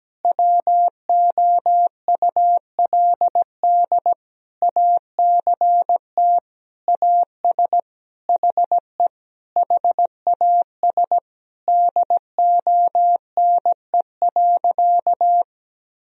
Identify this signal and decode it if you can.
Morse code, heard as a keyed tone, about 17 wpm, 700 Hz. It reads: WOULD ACT AS HE HAS DONE.
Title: Sentences from Top 1000 Words - Review Part 1 - 17wpm